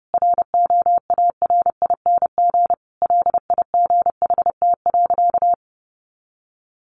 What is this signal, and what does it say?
Morse code, heard as a keyed tone, about 30 wpm, 700 Hz. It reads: ROARING LIGHT.
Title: A Sound Of Thunder 30wpm